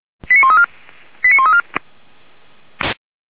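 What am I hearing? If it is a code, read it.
Five-tone selective call (ZVEI), heard as quick runs of five tones, tones of about 70 ms. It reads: 89134 89134
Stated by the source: Received on an SDR.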